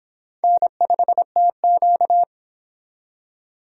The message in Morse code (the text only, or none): N5TQ